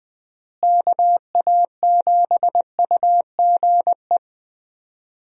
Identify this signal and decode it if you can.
Morse code, heard as a keyed tone, about 20 wpm, 700 Hz. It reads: KA7UGE
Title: USA Call Signs - 6 Characters - Repeated 3x - 20wpm